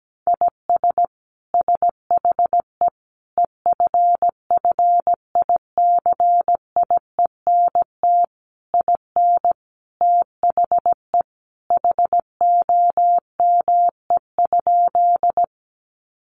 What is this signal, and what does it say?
Morse code, heard as a keyed tone, about 17 wpm, 700 Hz. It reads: IS SHE EFFICIENT IN THE HOME?
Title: Sentences from Top 1300 Words 17wpm